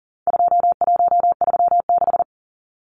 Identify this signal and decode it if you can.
Morse code, heard as a keyed tone, about 40 wpm, 700 Hz. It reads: 2136
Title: Numbers - 4-Digit 40wpm